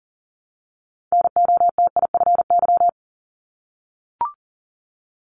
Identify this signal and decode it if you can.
Morse code, heard as a keyed tone, about 39 wpm, 700 Hz. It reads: NOTIFY
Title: Multi-letter Phonemes Repeated 3x 40wpm